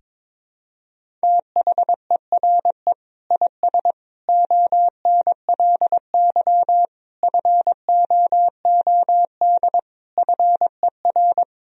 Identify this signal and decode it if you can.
Morse code, heard as a keyed tone, about 22 wpm, 700 Hz. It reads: THERE IS ONLY FOOD FER
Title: Sentences from Top 300 Words and Abbreviations 22wpm